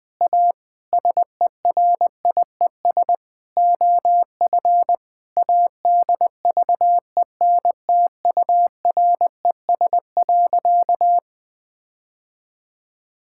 Morse code, heard as a keyed tone, about 20 wpm, 700 Hz. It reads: A SERIES OF ADVENTURES.
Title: Sentences with 7-Word Phrases - Spoken Beginning - 20wpm